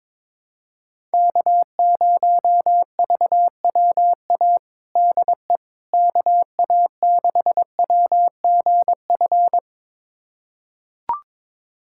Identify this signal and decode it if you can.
Morse code, heard as a keyed tone, about 21 wpm, 700 Hz. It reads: K0VWA DE KA6WGF
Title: USA Callsign DE Callsign 22wpm